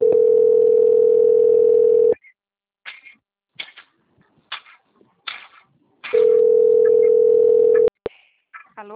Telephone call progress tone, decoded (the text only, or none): ringback tone